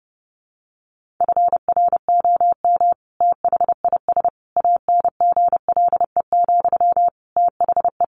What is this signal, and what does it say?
Morse code, heard as a keyed tone, about 30 wpm, 700 Hz. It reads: FROM THIS ANGLE, THE
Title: Sentences from Top 1500 Words 30wpm